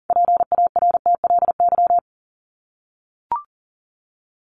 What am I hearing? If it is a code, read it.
Morse code, heard as a keyed tone, about 40 wpm, 700 Hz. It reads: PARTLY
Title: LY Suffix - Top 100 Words with Base Words 40wpm